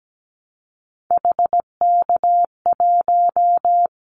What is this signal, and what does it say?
Morse code, heard as a keyed tone, about 17 wpm, 700 Hz. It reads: HK1
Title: Callsign Prefixes for Top 100 DXCC - 17wpm